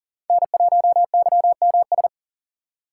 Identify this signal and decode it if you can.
Morse code, heard as a keyed tone, about 40 wpm, 700 Hz. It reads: N1YMS
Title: USA Call Signs - 5 Characters - 2x Character Spacing - 40wpm